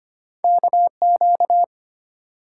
Morse code, heard as a keyed tone, about 25 wpm, 690 Hz. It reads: KQ
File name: Two Letter-Number Combinations - 25wpm